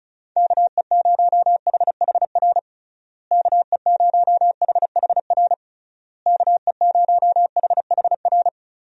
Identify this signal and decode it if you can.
Morse code, heard as a keyed tone, about 36 wpm, 700 Hz. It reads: KE0HHR KE0HHR KE0HHR
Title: USA Call Signs - 6 Characters - Repeated 3x - 35wpm